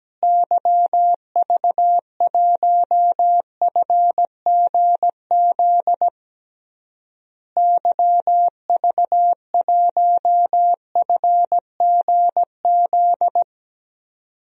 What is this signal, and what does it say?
Morse code, heard as a keyed tone, about 17 wpm, 700 Hz. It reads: YV1FGZ YV1FGZ
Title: DXCC-100 Call Signs - 6 Characters - Repeated 3x - 17wpm